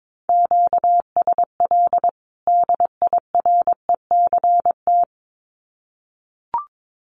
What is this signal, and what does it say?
Morse code, heard as a keyed tone, about 22 wpm, 700 Hz. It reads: QSL DIRECT
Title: Common QSO Phrases 22wpm